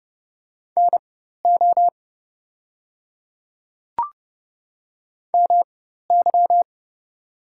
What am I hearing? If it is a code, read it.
Morse code, heard as a keyed tone, about 30 wpm, spaced as for 14 wpm, 700 Hz. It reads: NO MY
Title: 4x Character Spacing - Top 300 Words - 30wpm